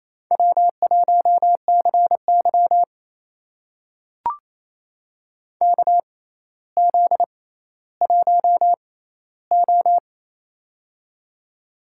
Morse code, heard as a keyed tone, about 28 wpm, spaced as for 14 wpm, 700 Hz. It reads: W1CY KZ1O
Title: USA Call Signs - 4 Characters - 6x Character Spacing - 28wpm